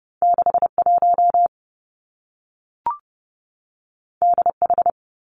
Morse code, heard as a keyed tone, about 30 wpm, 700 Hz. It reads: 6 1 D H
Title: Two Challenging Letter-Number Combinations 30wpm